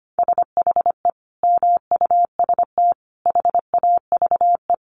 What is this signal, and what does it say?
Morse code, heard as a keyed tone, about 25 wpm, 700 Hz. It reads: SHE MUST HAVE
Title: Sentences from Top 1100 Words 25wpm